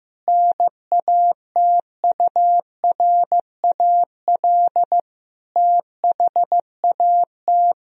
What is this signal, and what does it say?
Morse code, heard as a keyed tone, about 15 wpm, 700 Hz. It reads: NATURAL THAT
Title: Sentences from Top 1000 Words - Review Part 4 - 15wpm